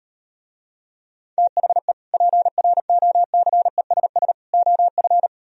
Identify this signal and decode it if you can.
Morse code, heard as a keyed tone, about 38 wpm, 700 Hz. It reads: THE PROCESS OF